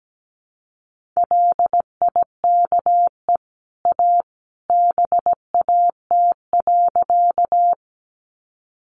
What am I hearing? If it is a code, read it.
Morse code, heard as a keyed tone, about 17 wpm, 700 Hz. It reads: LIKE A BAT.
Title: Sentences from Top 1500 Words 17wpm